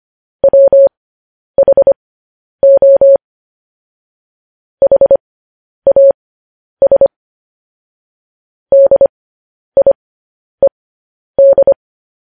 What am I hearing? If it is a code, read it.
Morse code, heard as a keyed tone, about 25 wpm, spaced as for 10 wpm, 550 Hz. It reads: WHO HAS DIED
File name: podcast / News headlines-1x-25x10